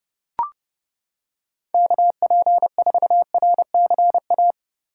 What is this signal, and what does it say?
Morse code, heard as a keyed tone, about 29 wpm, 700 Hz. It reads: KP4RCA